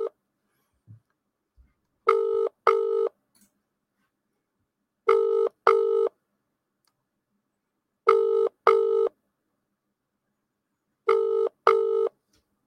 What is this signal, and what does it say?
Ringback tone (UK), 400 + 450 Hz.